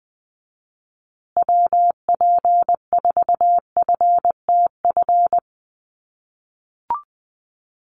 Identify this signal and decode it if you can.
Morse code, heard as a keyed tone, about 20 wpm, 700 Hz. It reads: WP4FTF
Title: USA Call Signs Repeated 3x 20wpm